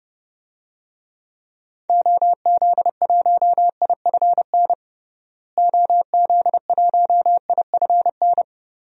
Morse code, heard as a keyed tone, about 32 wpm, 700 Hz. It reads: OZ1IFN OZ1IFN
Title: DXCC-100 Call Signs - 6 Characters - Repeated 3x - 30wpm